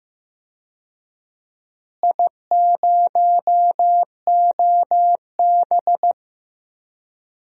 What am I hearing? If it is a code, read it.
Morse code, heard as a keyed tone, about 15 wpm, 700 Hz. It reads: I0OB
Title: DXCC-100 Call Signs - 4 Characters - Repeated 3x - 15wpm